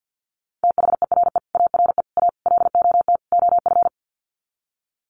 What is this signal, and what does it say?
Morse code, heard as a keyed tone, about 48 wpm, 700 Hz. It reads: THERE ARE A LOT OF
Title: Sets of 5 Words 50wpm